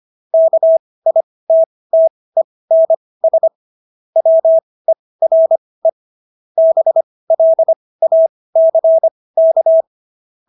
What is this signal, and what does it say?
Morse code, heard as a keyed tone, about 25 wpm, spaced as for 18 wpm, 650 Hz. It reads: KITTENS WERE BLACK